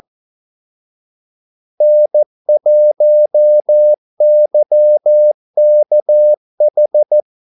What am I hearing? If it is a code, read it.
Morse code, heard as a keyed tone, about 14 wpm, 600 Hz. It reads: N1YKH
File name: Callsigns-14wpm